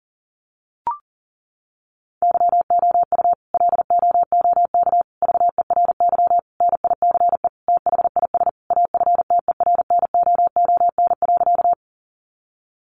Morse code, heard as a keyed tone, about 40 wpm, 700 Hz. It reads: YOU LOOK VERY NICE THIS AFTERNOON.
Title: Sentences from Top 1500 Words 40wpm